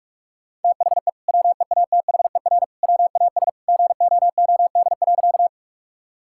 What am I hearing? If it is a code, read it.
Morse code, heard as a keyed tone, about 47 wpm, 700 Hz. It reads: THE WEATHER WAS GOOD.
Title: Sentences from Top 1000 Words - Review Part 5 - 45wpm